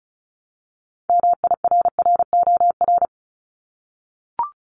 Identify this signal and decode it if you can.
Morse code, heard as a keyed tone, about 36 wpm, 700 Hz. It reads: MIRROR
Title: Multi-letter Phonemes 35wpm